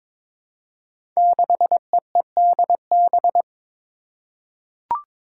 Morse code, heard as a keyed tone, about 22 wpm, 700 Hz. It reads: 6EEDB
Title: Five Challenging Letter-Number Combinations 22wpm